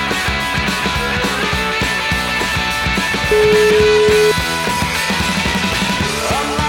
Ringback tone (Europe, 425 Hz), with noise.